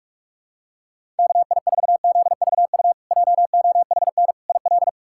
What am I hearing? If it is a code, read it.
Morse code, heard as a keyed tone, about 47 wpm, 700 Hz. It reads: KI4ZVU JOHN IL
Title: K1USN SST QSO Practice - 45wpm